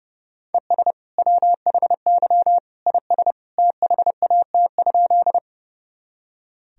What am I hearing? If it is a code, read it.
Morse code, heard as a keyed tone, about 32 wpm, 700 Hz. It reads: ES WHY IS THAT?
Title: Sentences from Top 200 Words and Abbreviations 30wpm